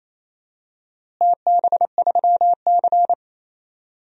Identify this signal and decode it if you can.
Morse code, heard as a keyed tone, about 28 wpm, 700 Hz. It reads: TB3C
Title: DXCC-100 Call Signs - 4 Characters - Repeated 2x - 28wpm